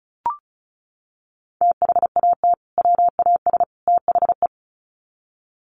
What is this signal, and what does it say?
Morse code, heard as a keyed tone, about 34 wpm, 700 Hz. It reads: THAT WAS THE
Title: Sets of 3 Words 35wpm